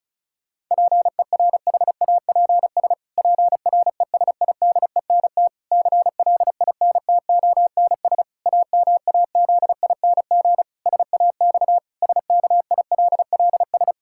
Morse code, heard as a keyed tone, about 34 wpm, 700 Hz. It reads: PERHAPS PRESIDENT CLINTONS AMAZING SAX SKILLS
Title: Pangram Challenge v2 - 35wpm